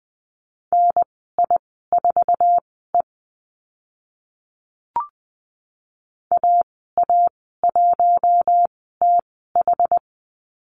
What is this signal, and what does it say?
Morse code, heard as a keyed tone, about 20 wpm, spaced as for 15 wpm, 700 Hz. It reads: NI4E AA1TH
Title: 062 Lesson 11g - Letter h - Call Signs 20wpm